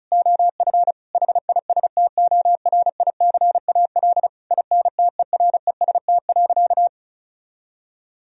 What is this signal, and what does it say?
Morse code, heard as a keyed tone, about 36 wpm, 700 Hz. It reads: OF HISTORICAL INTEREST.
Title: Sentences from Top 1000 Words - Review Part 4 - 35wpm